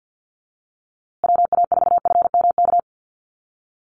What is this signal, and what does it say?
Morse code, heard as a keyed tone, about 50 wpm, 700 Hz. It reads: WA4FMK
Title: USA Call Signs Repeated 3x 50wpm